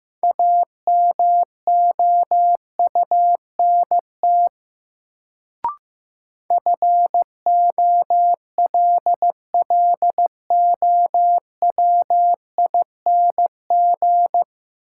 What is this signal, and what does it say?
Morse code, heard as a keyed tone, about 15 wpm, 700 Hz. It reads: AMOUNT FOLLOWING
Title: Sentences from Top 700 Words - New Word Drill - 15wpm